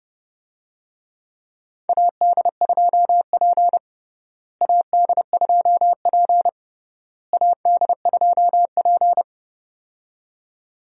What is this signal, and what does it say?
Morse code, heard as a keyed tone, about 29 wpm, 700 Hz. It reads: AD2P AD2P AD2P
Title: USA Call Signs - 4 Characters - Repeated 3x - 30wpm